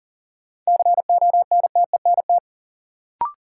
Morse code, heard as a keyed tone, about 39 wpm, 700 Hz. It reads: CONTENT
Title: Sentences from Top 900 Words - New Word Drill - 40wpm